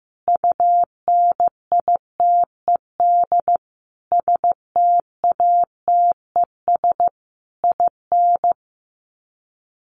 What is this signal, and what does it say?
Morse code, heard as a keyed tone, about 14 wpm, 700 Hz. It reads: UNITED STATES IN
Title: Sets of 5 Words Encore 15wpm